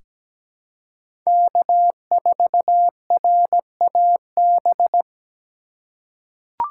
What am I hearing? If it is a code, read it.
Morse code, heard as a keyed tone, about 17 wpm, 700 Hz. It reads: K4RAB
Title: K1USN SST QSO Practice - 17wpm